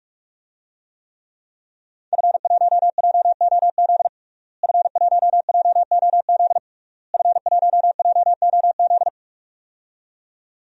Morse code, heard as a keyed tone, about 48 wpm, 700 Hz. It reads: F1JOZ F1JOZ F1JOZ
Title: DXCC-100 Call Signs - 5 Characters - Repeated 3x - 45wpm